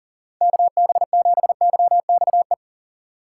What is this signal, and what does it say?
Morse code, heard as a keyed tone, about 38 wpm, 700 Hz. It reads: KB7YXE
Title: USA Call Signs - 6 Characters - 4x Character Spacing - 40wpm